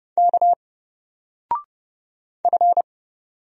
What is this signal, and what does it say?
Morse code, heard as a keyed tone, about 30 wpm, 700 Hz. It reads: K F
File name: Single-Letter-Number-30wpm